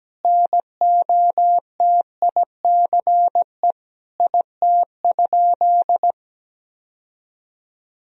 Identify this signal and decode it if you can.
Morse code, heard as a keyed tone, about 17 wpm, 700 Hz. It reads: NOTICE IT?
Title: Sentences from Top 1000 Words - Review Part 2 - 17wpm